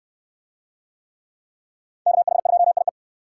Morse code, heard as a keyed tone, about 69 wpm, 700 Hz. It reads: KH2IE